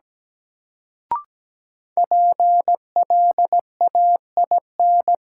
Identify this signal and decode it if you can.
Morse code, heard as a keyed tone, about 17 wpm, 700 Hz. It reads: PLAIN
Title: Sentences from Top 400 Words - New Word Drill - 17wpm